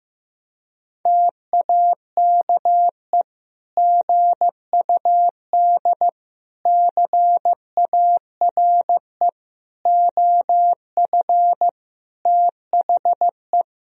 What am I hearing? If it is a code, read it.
Morse code, heard as a keyed tone, about 15 wpm, 700 Hz. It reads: TAKE GUD CARE OF THE
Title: Sentences from Top 400 Words and Abbreviations 15wpm